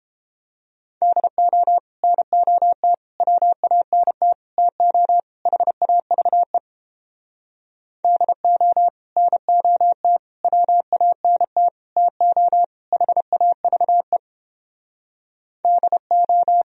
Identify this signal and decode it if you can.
Morse code, heard as a keyed tone, about 29 wpm, 700 Hz. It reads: DO NOT WANT TO HAVE DO NOT WANT TO HAVE DO